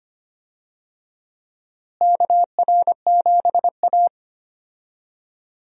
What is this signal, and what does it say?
Morse code, heard as a keyed tone, about 25 wpm, 700 Hz. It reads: KR7A
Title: USA Call Signs - 4 Characters - Repeated 2x - 25wpm